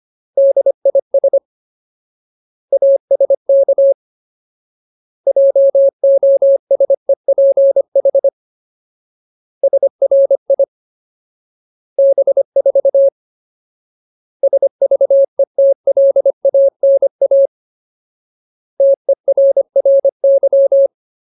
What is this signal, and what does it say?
Morse code, heard as a keyed tone, about 25 wpm, 550 Hz. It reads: DIS ASK JOSEPH SRI B4 SVETLANA TERRY
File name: Ultimate Word List / 25WPM Ultimate